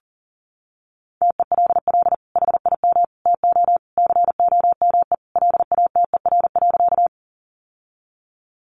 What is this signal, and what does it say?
Morse code, heard as a keyed tone, about 40 wpm, 700 Hz. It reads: TELL HIM TO COME LATER.
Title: Sentences from Top 700 Words 40wpm